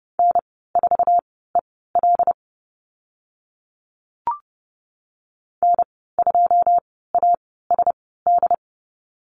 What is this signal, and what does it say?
Morse code, heard as a keyed tone, about 30 wpm, spaced as for 17 wpm, 700 Hz. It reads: N4EL N2ASD
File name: Lesson 14g - Number 2 - Call Signs 30wpm